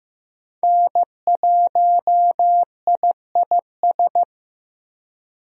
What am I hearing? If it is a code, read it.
Morse code, heard as a keyed tone, about 15 wpm, 700 Hz. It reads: N1IIS
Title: USA Call Signs - 5 Characters - Repeated 3x - 15wpm